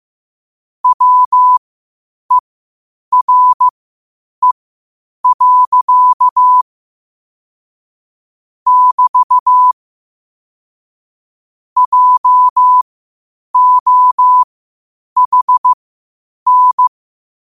Morse code, heard as a keyed tone, about 15 wpm, spaced as for 8 wpm, 1000 Hz. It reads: WERE. <BT> JOHN
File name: qotdfiles / 8wpm